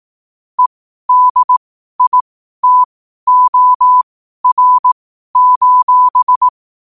Morse code, heard as a keyed tone, about 18 wpm, spaced as for 12 wpm, 1000 Hz. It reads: EDITOR:
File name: qotdfiles / 12wpm